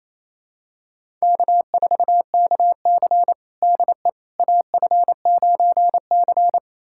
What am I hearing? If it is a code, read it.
Morse code, heard as a keyed tone, about 28 wpm, 700 Hz. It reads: K4KC DE AF9C
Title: USA Callsign DE Callsign - 4 Characters - 28wpm